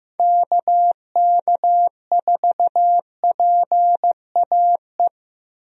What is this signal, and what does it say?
Morse code, heard as a keyed tone, about 16 wpm, 700 Hz. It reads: KK4PAE
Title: USA Call Signs - 6 Characters - Repeated 3x - 15wpm